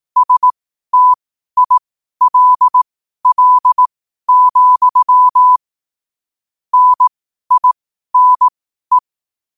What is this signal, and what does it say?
Morse code, heard as a keyed tone, about 18 wpm, spaced as for 12 wpm, 1000 Hz. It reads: STILL, NINE
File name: qotdfiles / 12wpm